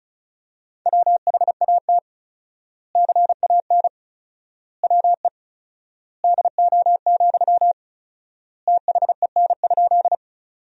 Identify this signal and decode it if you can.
Morse code, heard as a keyed tone, about 34 wpm, 700 Hz. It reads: WHAT CAN WE DO, THEN?
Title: 4x Word-Spacing - Sentences from Top 100 Words - 35wpm